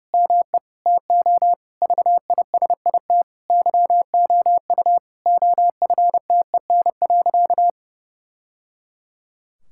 Morse code, heard as a keyed tone, about 30 wpm, 700 Hz. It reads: ME TO VISIT YOU OFTEN.
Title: Sentences from Top 1000 Words - Review Part 6 - 30wpm